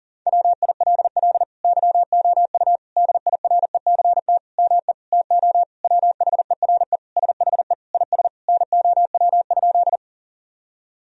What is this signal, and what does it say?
Morse code, heard as a keyed tone, about 40 wpm, 700 Hz. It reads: WILL YOU DIRECT ME TO WHERE SHE IS NOW?
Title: Sentences from Top 1000 Words - Review Part 2 - 40wpm